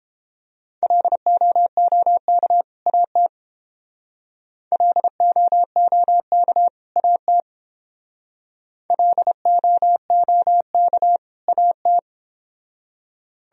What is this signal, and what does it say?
Morse code, heard as a keyed tone, about 30 wpm, 700 Hz. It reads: LOOK AT LOOK AT LOOK AT